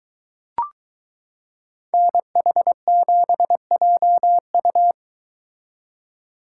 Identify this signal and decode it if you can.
Morse code, heard as a keyed tone, about 22 wpm, 700 Hz. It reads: NH7JU